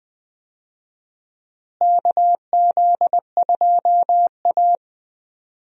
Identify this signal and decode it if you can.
Morse code, heard as a keyed tone, about 20 wpm, 700 Hz. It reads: KZ2A
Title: USA Call Signs - 4 Characters - Repeated 2x - 20wpm